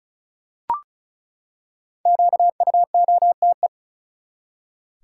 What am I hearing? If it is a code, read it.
Morse code, heard as a keyed tone, about 34 wpm, 700 Hz. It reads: QUOTE
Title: Five Letter Words - 35wpm